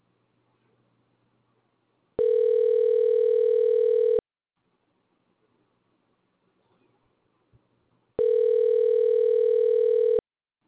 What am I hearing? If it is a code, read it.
Ringback tone (North America), 440 + 480 Hz.